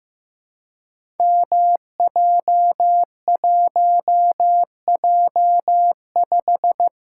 Morse code, heard as a keyed tone, about 15 wpm, 700 Hz. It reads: MJ1J5
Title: Code Groups - 5 Letter:Numbers - 15wpm